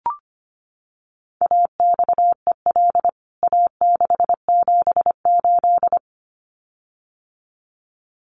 Morse code, heard as a keyed tone, about 25 wpm, 700 Hz. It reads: AXEL A678
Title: CWT Exchange 25wpm